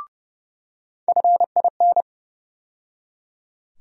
Morse code, heard as a keyed tone, about 30 wpm, 700 Hz. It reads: FIN
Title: USA Call Sign Postfixes - 3 Letters 30wpm